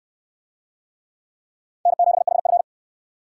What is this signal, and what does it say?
Morse code, heard as a keyed tone, about 68 wpm, 700 Hz. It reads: N7HF